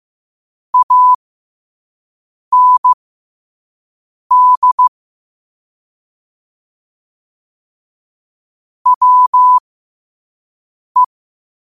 Morse code, heard as a keyed tone, about 15 wpm, spaced as for 5 wpm, 1000 Hz. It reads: AND WE
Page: New quotes every day in morse code at 5 Words per minute.